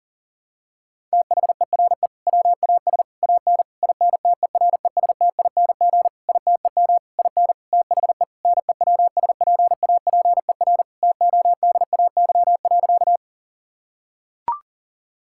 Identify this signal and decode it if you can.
Morse code, heard as a keyed tone, about 40 wpm, 700 Hz. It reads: THERE WAS AN INTERESTING ITEM IN THE NEWSPAPER TODAY.
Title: Sentences from Top 1000 Words - Review Part 6 - 40wpm